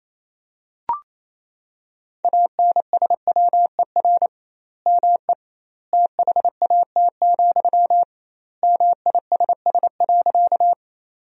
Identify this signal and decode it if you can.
Morse code, heard as a keyed tone, about 28 wpm, 700 Hz. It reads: ANSWER ME THAT, MISS.
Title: 2x Word-Spacing - Sentences from Top 500 Words - 28wpm